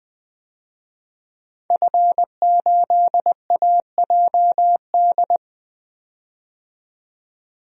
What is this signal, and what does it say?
Morse code, heard as a keyed tone, about 20 wpm, 700 Hz. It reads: F8AJD
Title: DXCC-100 Call Signs - 5 Characters - 20wpm